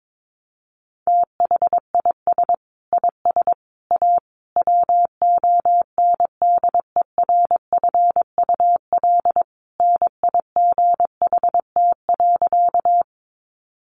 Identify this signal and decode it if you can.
Morse code, heard as a keyed tone, about 22 wpm, 700 Hz. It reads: THIS IS A WONDERFUL NIGHT.
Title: Sentences from Top 1100 Words 22wpm